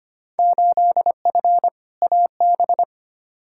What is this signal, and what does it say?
Morse code, heard as a keyed tone, about 25 wpm, 700 Hz. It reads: 8F AB
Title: Field Day Exchanges Encore - 25wpm